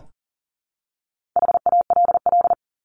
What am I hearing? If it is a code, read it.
Morse code, heard as a keyed tone, about 38 wpm, 700 Hz. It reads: HALL
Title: Top 1751-2000 Words 40wpm